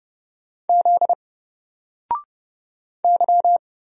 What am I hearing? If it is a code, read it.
Morse code, heard as a keyed tone, about 31 wpm, 700 Hz. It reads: Z Y
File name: Single-Letter-Number-30wpm